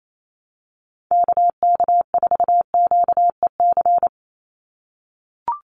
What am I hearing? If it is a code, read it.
Morse code, heard as a keyed tone, about 28 wpm, 700 Hz. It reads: KK4QEC
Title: USA Call Signs 28wpm